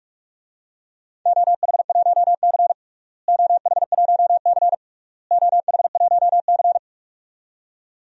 Morse code, heard as a keyed tone, about 46 wpm, 700 Hz. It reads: OH1C OH1C OH1C
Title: DXCC-100 Call Signs - 4 Characters - Repeated 3x - 45wpm